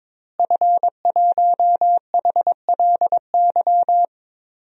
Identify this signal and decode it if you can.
Morse code, heard as a keyed tone, about 22 wpm, 700 Hz. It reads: F1HLY
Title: DXCC-100 Call Signs - 5 Characters - Repeated 3x - 22wpm